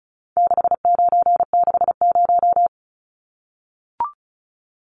Morse code, heard as a keyed tone, about 34 wpm, 700 Hz. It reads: 6960
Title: Numbers - 4-Digit 35wpm